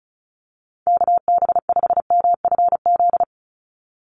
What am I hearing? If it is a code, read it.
Morse code, heard as a keyed tone, about 34 wpm, 700 Hz. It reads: KB5MFZ
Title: USA Call Signs 35wpm